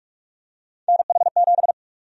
Morse code, heard as a keyed tone, about 46 wpm, 700 Hz. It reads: NH7
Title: USA Call Sign Prefixes - 45wpm